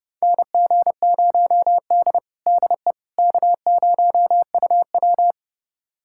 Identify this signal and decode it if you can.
Morse code, heard as a keyed tone, about 30 wpm, 700 Hz. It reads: NG0D DE K0UW
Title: USA Callsign DE Callsign - 4 Characters - 30wpm